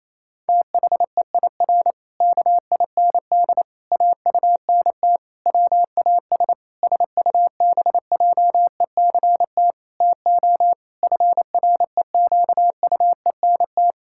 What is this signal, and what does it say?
Morse code, heard as a keyed tone, about 28 wpm, 700 Hz. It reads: THEIR KIND AUNT WAS SUBJECT TO FREQUENT
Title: Pangram Challenge v2 - 28wpm